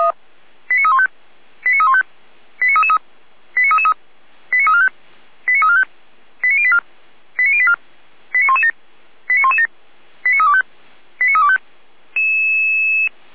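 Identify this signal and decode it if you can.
Five-tone selective call (ZVEI), heard as quick runs of five tones, tones of about 70 ms. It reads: 89315 89315 89333 89333 89345 89345 89084 89084 89117 89117 89325 89325
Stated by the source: Five-tone ZVEI selective call received on an SDR.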